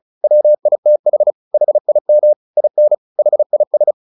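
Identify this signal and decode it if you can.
Morse code, heard as a keyed tone, about 34 wpm, 600 Hz. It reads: WITH HIM IN HIS
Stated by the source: Morse practice at 35 wpm from CW med Gnister, sending Ken Follett's novel Never in English.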